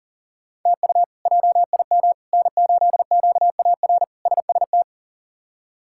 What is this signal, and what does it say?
Morse code, heard as a keyed tone, about 40 wpm, 700 Hz. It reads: TU JIM N8QAR SST
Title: K1USN SST QSO Practice - 40wpm